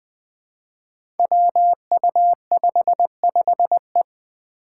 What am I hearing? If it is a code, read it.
Morse code, heard as a keyed tone, about 20 wpm, 700 Hz. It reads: WU55E
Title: Five Challenging Letter-Number Combinations 20wpm